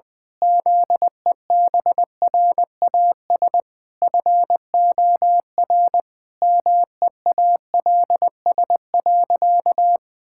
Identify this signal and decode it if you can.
Morse code, heard as a keyed tone, about 20 wpm, 700 Hz. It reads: ZEBRAS FOR MEALS.
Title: Pangram Challenge v2 - 20wpm